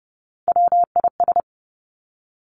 Morse code, heard as a keyed tone, about 30 wpm, 700 Hz. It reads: WIS
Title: Three Challenging Letter-Number Combinations 30wpm